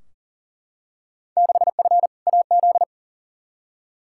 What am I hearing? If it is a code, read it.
Morse code, heard as a keyed tone, about 40 wpm, 700 Hz. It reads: 6F AZ
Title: Field Day Exchanges Encore - 40wpm